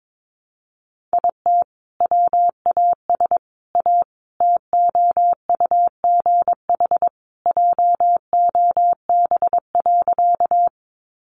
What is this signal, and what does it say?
Morse code, heard as a keyed tone, about 22 wpm, 700 Hz. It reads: IT WAS A TOUGH JOB.
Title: Sentences from Top 1200 Words 22wpm